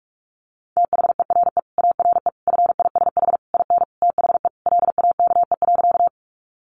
Morse code, heard as a keyed tone, about 44 wpm, 700 Hz. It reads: THERE ARE FISH IN THE LAKE.
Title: Sentences from Top 1100 Words 45wpm